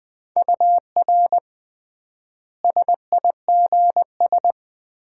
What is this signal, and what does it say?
Morse code, heard as a keyed tone, about 21 wpm, 700 Hz. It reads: UR SIGS
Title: Common QSO Phrases - 3x Word-Spacing - 20wpm